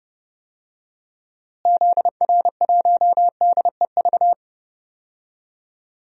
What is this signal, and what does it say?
Morse code, heard as a keyed tone, about 32 wpm, 700 Hz. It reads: ZR1DEV
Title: DXCC-100 Call Signs - 6 Characters - Repeated 3x - 30wpm